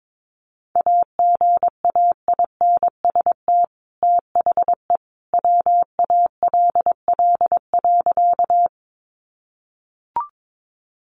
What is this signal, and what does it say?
Morse code, heard as a keyed tone, about 22 wpm, 700 Hz. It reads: AGAINST THE WALL.
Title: Sentences from Top 1100 Words 22wpm